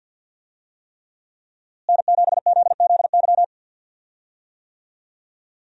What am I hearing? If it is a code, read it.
Morse code, heard as a keyed tone, about 48 wpm, 700 Hz. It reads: N7ZZY
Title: USA Call Signs - 5 Characters - Repeated 3x - 50wpm